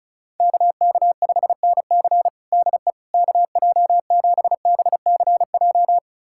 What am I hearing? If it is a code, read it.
Morse code, heard as a keyed tone, about 34 wpm, 700 Hz. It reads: KK5NC DE KJ7BCJ
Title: USA Callsign DE Callsign 35wpm